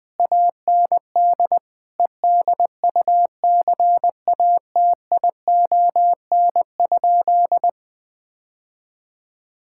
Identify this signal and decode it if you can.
Morse code, heard as a keyed tone, about 19 wpm, 700 Hz. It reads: AND EDUCATION?
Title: Sentences from Top 1000 Words - Review Part 6 - 20wpm